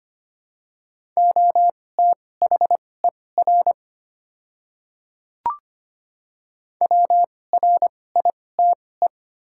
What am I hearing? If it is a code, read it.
Morse code, heard as a keyed tone, about 25 wpm, spaced as for 18 wpm, 690 Hz. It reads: OTHER WRITE
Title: 2x Character Spacing - Top 100 Words - 25wpm